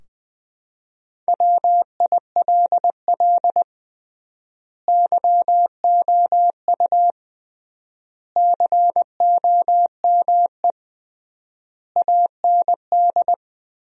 Morse code, heard as a keyed tone, about 20 wpm, 700 Hz. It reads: WILL YOU COME AND
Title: 3x Word-Spacing - Sentences from Top 100 Words - 20wpm